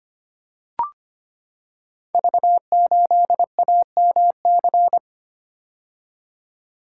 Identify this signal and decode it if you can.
Morse code, heard as a keyed tone, about 25 wpm, 700 Hz. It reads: V8AMC
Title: Code Groups - 5 Letter:Numbers - 25wpm